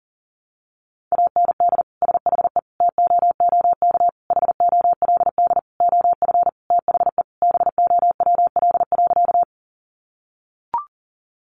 Morse code, heard as a keyed tone, about 40 wpm, 700 Hz. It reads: AND SHE TOOK HOLD OF THE BOWL.
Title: Sentences from Top 1200 Words 40wpm